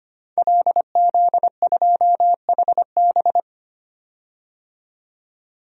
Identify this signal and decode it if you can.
Morse code, heard as a keyed tone, about 25 wpm, 700 Hz. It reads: LZ2HB
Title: DXCC-100 Call Signs - 5 Characters - 25wpm